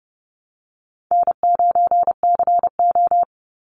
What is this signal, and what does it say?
Morse code, heard as a keyed tone, about 30 wpm, 700 Hz. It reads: N9CO
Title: Easy Call Signs Repeated 2x - 30wpm